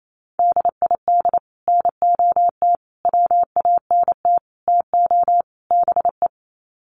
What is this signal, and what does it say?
Morse code, heard as a keyed tone, about 28 wpm, 700 Hz. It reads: DID NOT WANT TO BE
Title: Sets of 5 Words Encore 28wpm